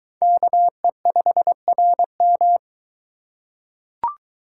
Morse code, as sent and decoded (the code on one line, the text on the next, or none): -.- . ..... .-. --
KE5RM